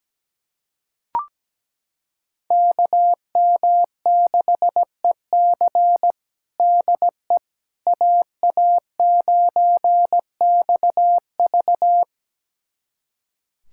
Morse code, heard as a keyed tone, about 17 wpm, 700 Hz. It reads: KM6EC DE AA9XV
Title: USA Callsign DE Callsign - 5 Characters - 17wpm